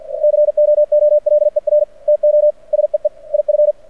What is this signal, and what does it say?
Morse code, heard as a keyed tone, about 50 wpm, 600 Hz. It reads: COOLER TO SEE IF
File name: archived / CRSnet